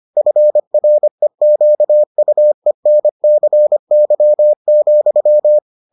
Morse code, heard as a keyed tone, about 25 wpm, 600 Hz. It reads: FREQUENCY,